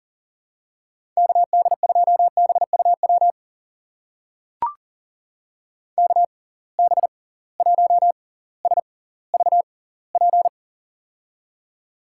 Morse code, heard as a keyed tone, about 39 wpm, spaced as for 20 wpm, 700 Hz. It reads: KD2BUW KB1SVP
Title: USA Call Signs - 6 Characters - 6x Character Spacing - 40wpm